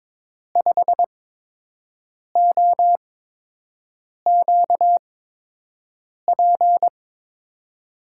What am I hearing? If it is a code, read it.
Morse code, heard as a keyed tone, about 22 wpm, spaced as for 6 wpm, 700 Hz. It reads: 5OQP
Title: USA Call Signs - 6 Characters - 8x Character Spacing - 22wpm